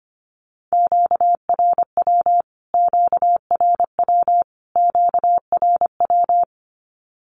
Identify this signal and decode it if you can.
Morse code, heard as a keyed tone, about 25 wpm, 700 Hz. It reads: QRW QRW QRW
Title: Q-Codes Repeated 3x 25wpm